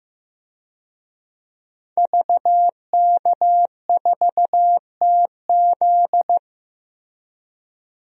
Morse code, heard as a keyed tone, about 15 wpm, 700 Hz. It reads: VK4TZ
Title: DXCC-100 Call Signs - 5 Characters - 15wpm